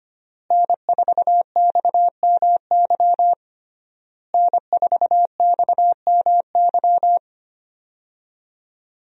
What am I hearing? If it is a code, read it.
Morse code, heard as a keyed tone, about 25 wpm, 700 Hz. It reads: N4XMY N4XMY
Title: USA Call Signs - 5 Characters - Repeated 3x - 25wpm